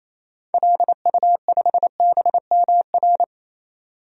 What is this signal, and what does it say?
Morse code, heard as a keyed tone, about 28 wpm, 700 Hz. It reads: LU5BMR
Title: DXCC-100 Call Signs - 6 Characters - 28wpm